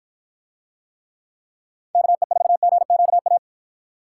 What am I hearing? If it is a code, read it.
Morse code, heard as a keyed tone, about 52 wpm, 700 Hz. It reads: KE4GQA